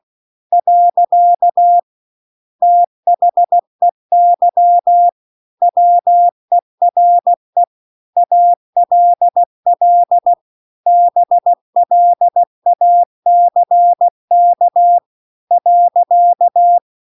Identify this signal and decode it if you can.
Morse code, heard as a keyed tone, about 16 wpm, 700 Hz. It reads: . THEY WERE ALL BLACK .